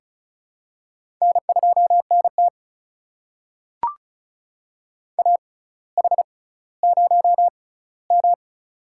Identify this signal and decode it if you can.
Morse code, heard as a keyed tone, about 36 wpm, spaced as for 18 wpm, 700 Hz. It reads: N2NT AH0M
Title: USA Call Signs - 4 Characters - 6x Character Spacing - 35wpm